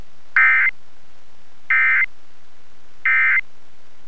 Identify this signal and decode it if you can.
Emergency Alert System (SAME) tones, heard as harsh buzzing bursts. It is heard from 0.3 to 3.4 s.